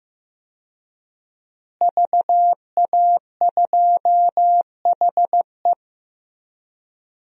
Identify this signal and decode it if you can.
Morse code, heard as a keyed tone, about 15 wpm, 700 Hz. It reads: VA2HE
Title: DXCC-100 Call Signs - 5 Characters - Repeated 2x - 15wpm